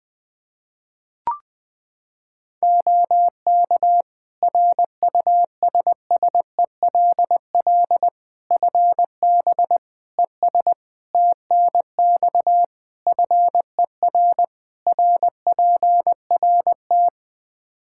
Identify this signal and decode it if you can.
Morse code, heard as a keyed tone, about 20 wpm, 700 Hz. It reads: OK RUSSELL FB ES TNX FER RPRT
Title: Practice QSO - Exchange 2 - Info - 20wpm